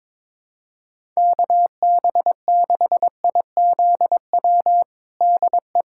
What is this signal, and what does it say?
Morse code, heard as a keyed tone, about 21 wpm, 700 Hz. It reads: KB6IZW DE
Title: USA Callsign DE Callsign 22wpm